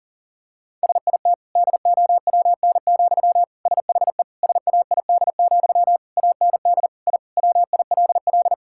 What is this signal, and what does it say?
Morse code, heard as a keyed tone, about 40 wpm, 700 Hz. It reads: SIT DOWN, SHE SAID, AND I WILL
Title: Sentences from Top 1000 Words - Review Part 6 - 40wpm